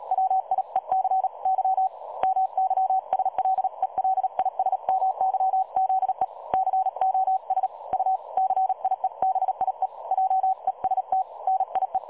Morse code, heard as a keyed tone, about 36 wpm, 760 Hz. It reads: WIEC Y MYSLELISMYZE COS U CIEBIE JEST NIE
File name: dlazaawansowanych / nagraniazpasma / websdr recording 3565.0kHz